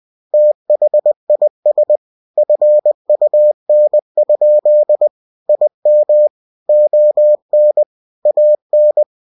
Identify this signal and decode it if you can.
Morse code, heard as a keyed tone, about 20 wpm, 600 Hz. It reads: THIS FUN? IM ON AN
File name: cw-books / winnie-the-pooh / winnie-20wpm-page-0096